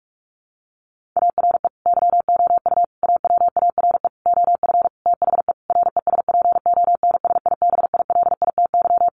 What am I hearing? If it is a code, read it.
Morse code, heard as a keyed tone, about 44 wpm, 700 Hz. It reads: ARE YOU AWARE OF THE RESPONSIBILITY
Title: Sentences from Top 1500 Words 45wpm